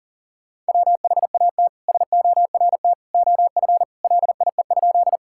Morse code, heard as a keyed tone, about 40 wpm, 700 Hz. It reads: WHAT SORT OF LIE?
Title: Sentences from Top 1600 Words 40wpm